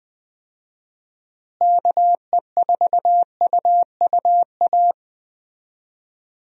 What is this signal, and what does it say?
Morse code, heard as a keyed tone, about 20 wpm, 700 Hz. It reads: KE4UUA
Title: USA Call Signs - 6 Characters - Repeated 2x - 20wpm